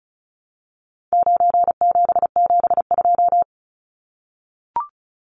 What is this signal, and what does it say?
Morse code, heard as a keyed tone, about 34 wpm, 700 Hz. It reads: 9772
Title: Numbers - 4-Digit 35wpm